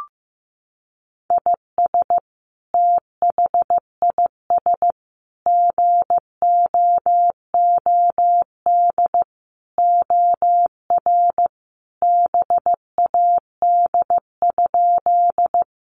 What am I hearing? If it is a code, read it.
Morse code, heard as a keyed tone, about 15 wpm, 700 Hz. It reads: IS THIS GOOD OR BAD?
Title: Sentences from Top 700 Words 15wpm